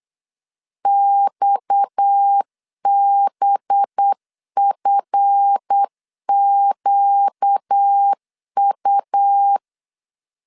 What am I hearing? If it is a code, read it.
Morse code, heard as a keyed tone, about 8 wpm, 800 Hz. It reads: XBFQU
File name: LEZIONI / 8 YJ45 / 8 YJ 8wpm 3313